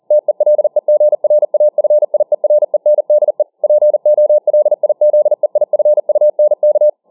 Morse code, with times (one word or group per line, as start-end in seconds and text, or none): 0.1-3.4 TELEGRAFIERENDE
3.6-6.9 POLIZEIFUNK